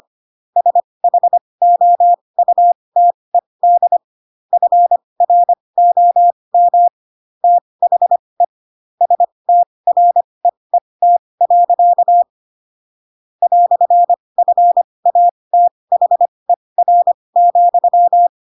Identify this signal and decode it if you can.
Morse code, heard as a keyed tone, about 25 wpm, spaced as for 20 wpm, 700 Hz. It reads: SHOUTED FROM THE STREET. "FATHER,
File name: Pinocchio / Pinocchio ch7 15 25 ik1hge